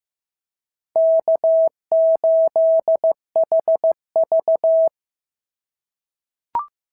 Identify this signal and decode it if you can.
Morse code, heard as a keyed tone, about 14 wpm, 650 Hz. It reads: K8HV
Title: SOTA QSO Practice 15wpm